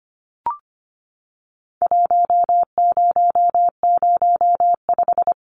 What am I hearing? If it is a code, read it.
Morse code, heard as a keyed tone, about 25 wpm, 700 Hz. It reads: 1005
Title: Numbers - Contest Style - 4-Digit 25wpm